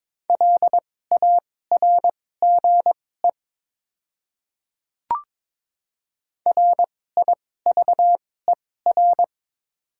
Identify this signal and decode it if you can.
Morse code, heard as a keyed tone, about 23 wpm, spaced as for 16 wpm, 700 Hz. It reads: LARGE RIVER
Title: 2x Character Spacing - Top 500 Words - 22wpm